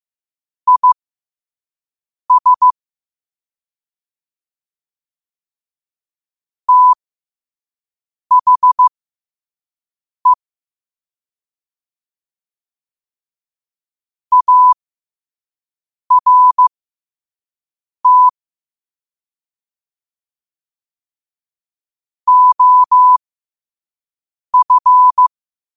Morse code, heard as a keyed tone, about 15 wpm, spaced as for 5 wpm, 1000 Hz. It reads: IS THE ART OF
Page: Quotes for Tue, 12 Aug 2025 in Morse Code at 5 words per minute.